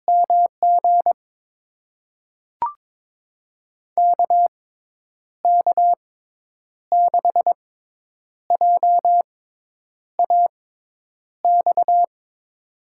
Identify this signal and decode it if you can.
Morse code, heard as a keyed tone, about 22 wpm, spaced as for 9 wpm, 700 Hz. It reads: MG KK6JAX
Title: USA Call Signs - 6 Characters - 6x Character Spacing - 22wpm